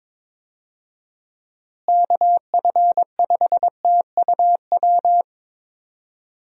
Morse code, heard as a keyed tone, about 21 wpm, 700 Hz. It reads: KF5TUW
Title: USA Call Signs - 6 Characters - Repeated 3x - 22wpm